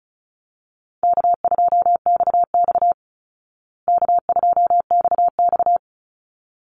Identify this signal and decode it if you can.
Morse code, heard as a keyed tone, about 34 wpm, 700 Hz. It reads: K2XX K2XX
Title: Easy Call Signs Repeated 2x - 35wpm